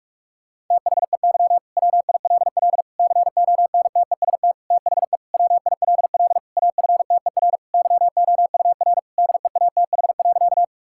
Morse code, heard as a keyed tone, about 41 wpm, 700 Hz. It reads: THEY WILL CONTEST THE WILL AFTER YOUR DEATH.
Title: Sentences from Top 1600 Words 45wpm